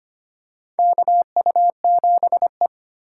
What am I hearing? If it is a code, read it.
Morse code, heard as a keyed tone, about 25 wpm, 700 Hz. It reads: KU7E
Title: USA Call Signs - 4x Character Spacing - 25wpm